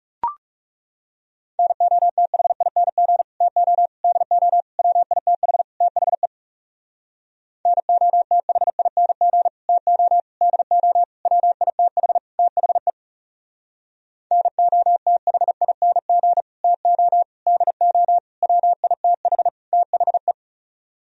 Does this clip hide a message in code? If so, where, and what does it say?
Yes: Morse code from 1.6 to 20.3 s, heard as a keyed tone, about 38 wpm, 700 Hz. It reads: NOTHING TO DO WITH THE NOTHING TO DO WITH THE NOTHING TO DO WITH THE